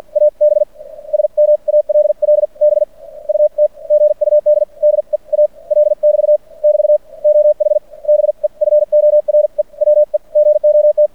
Morse code, heard as a keyed tone, about 50 wpm, spaced as for 38 wpm, 590 Hz. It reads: AB SMALL BUT GUD DEAL <BT> <BT> OH BEFORE WE GOT
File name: archived / CRSnet